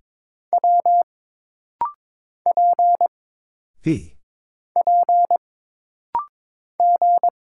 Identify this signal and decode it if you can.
Morse code, heard as a keyed tone, about 22 wpm, 700 Hz. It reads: W P P G
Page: This rapid-fire timing is geared to push your ICR (Instant Character Recognition) skills.